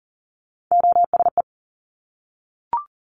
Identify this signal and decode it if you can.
Morse code, heard as a keyed tone, about 38 wpm, 700 Hz. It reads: OSE
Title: Top Three Letter Combinations in Words 40wpm